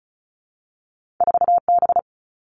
Morse code, heard as a keyed tone, about 36 wpm, 700 Hz. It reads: 4B
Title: All Cut Numbers - 2-Digit 35wpm